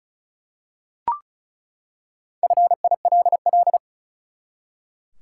Morse code, heard as a keyed tone, about 34 wpm, 700 Hz. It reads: FILL
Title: Four Letter Words - 35wpm